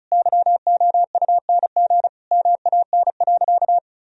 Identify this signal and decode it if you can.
Morse code, heard as a keyed tone, about 36 wpm, 700 Hz. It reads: YOUNG MAN.
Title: Sentences from Top 400 Words and Abbreviations 35wpm